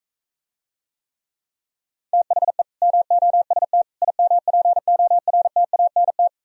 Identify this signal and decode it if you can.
Morse code, heard as a keyed tone, about 43 wpm, 700 Hz. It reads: THE MOST IMPORTANT